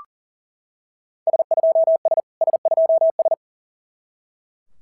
Morse code, heard as a keyed tone, about 41 wpm, 690 Hz. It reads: S2S S2S
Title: SOTA QSO Practice 40wpm